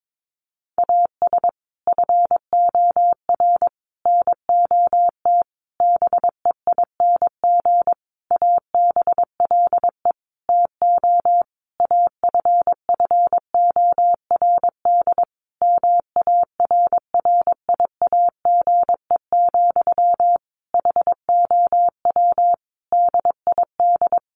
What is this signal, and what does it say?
Morse code, heard as a keyed tone, about 22 wpm, 700 Hz. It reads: AS FOR NOT BEING ABLE TO AFFORD MARRIAGE, HOW DID
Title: Sentences from Top 1100 Words 22wpm